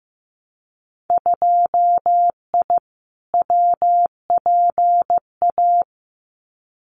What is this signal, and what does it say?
Morse code, heard as a keyed tone, about 15 wpm, 700 Hz. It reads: 2I WPA
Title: Winter Field Day Exchanges 15wpm